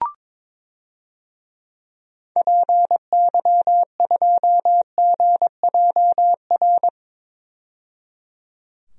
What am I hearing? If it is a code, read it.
Morse code, heard as a keyed tone, about 22 wpm, 700 Hz. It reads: PY2GJR
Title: DXCC-100 Call Signs - 6 Characters - 22wpm